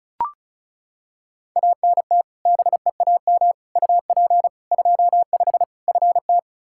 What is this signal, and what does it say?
Morse code, heard as a keyed tone, about 34 wpm, 700 Hz. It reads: ANT BEAM UP 25 FT
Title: K1USN SST QSO Practice 35wpm